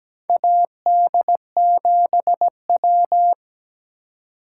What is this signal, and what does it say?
Morse code, heard as a keyed tone, about 17 wpm, 700 Hz. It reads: AD7W
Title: USA Call Signs - 2x Character Spacing - 17wpm